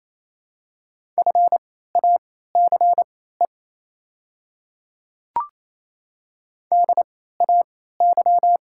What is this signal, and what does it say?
Morse code, heard as a keyed tone, about 28 wpm, spaced as for 16 wpm, 700 Hz. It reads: FACE DAY
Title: 3x Character Spacing - Top 500 Words - 28wpm